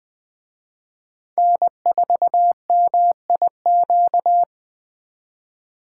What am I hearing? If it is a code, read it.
Morse code, heard as a keyed tone, about 24 wpm, 700 Hz. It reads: N4MIQ
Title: Speed Racing - USA 1x3 Call Signs 20wpm